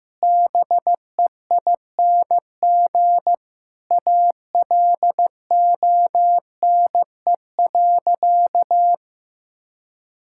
Morse code, heard as a keyed tone, about 16 wpm, 700 Hz. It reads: BEING ALONE.
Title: Sentences from Top 1000 Words - Review Part 6 - 15wpm